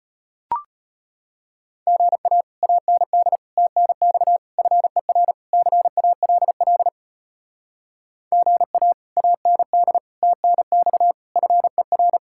Morse code, heard as a keyed tone, about 33 wpm, 700 Hz. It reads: GA AND TNX FER CALL GA AND TNX FER